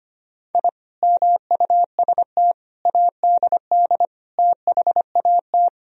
Morse code, heard as a keyed tone, about 25 wpm, 700 Hz. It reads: I MUST ADD THAT
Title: Sentences from Top 1000 Words - Review Part 1 - 25wpm